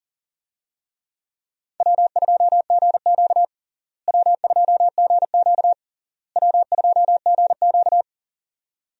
Morse code, heard as a keyed tone, about 37 wpm, 700 Hz. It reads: W2GQ W2GQ W2GQ
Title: USA Call Signs - 4 Characters - Repeated 3x - 40wpm